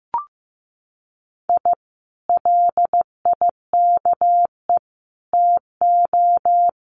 Morse code, heard as a keyed tone, about 15 wpm, 700 Hz. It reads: I LIKE TO
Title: Common QSO Phrases Encore 15wpm